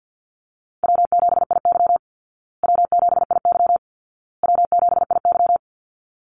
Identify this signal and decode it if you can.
Morse code, heard as a keyed tone, about 50 wpm, 700 Hz. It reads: W7IY W7IY W7IY
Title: Call Signs Repeated 2x - 50wpm